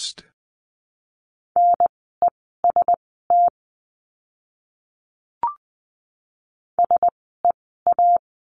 Morse code, heard as a keyed tone, about 20 wpm, spaced as for 14 wpm, 700 Hz. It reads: NEST SEA
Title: 036 Lesson 7f - Letter s - Words 20wpm